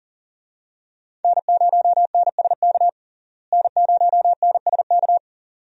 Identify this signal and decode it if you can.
Morse code, heard as a keyed tone, about 40 wpm, 700 Hz. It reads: N0NSK N0NSK
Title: USA Call Signs - 5 Characters - Repeated 2x - 40wpm